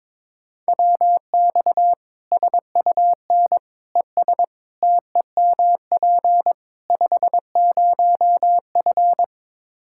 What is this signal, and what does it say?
Morse code, heard as a keyed tone, about 22 wpm, 700 Hz. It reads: WX SUN ES TEMP 50F
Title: Practice QSO - Exchange 2 - Info - 22wpm